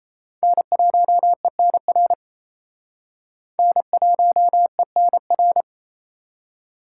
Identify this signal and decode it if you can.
Morse code, heard as a keyed tone, about 30 wpm, 700 Hz. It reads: N1ENR N1ENR
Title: Speed Racing - USA 1x3 Call Signs 28wpm